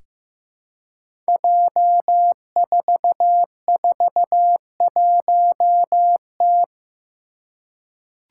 Morse code, heard as a keyed tone, about 15 wpm, 700 Hz. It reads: J441T
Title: Five Challenging Letter-Number Combinations 15wpm